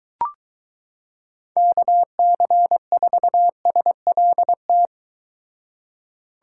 Morse code, heard as a keyed tone, about 23 wpm, 700 Hz. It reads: KC4SLT